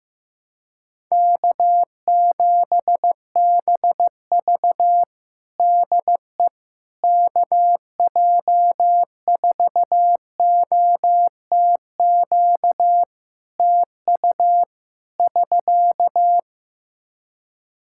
Morse code, heard as a keyed tone, about 15 wpm, 700 Hz. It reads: K7BV DE KJ4OTQ TU <SK>
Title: Practice QSO - Exchange 4 - Ending - 15wpm